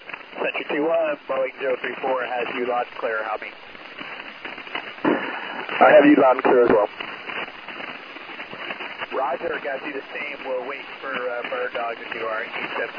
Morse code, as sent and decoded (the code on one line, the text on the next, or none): none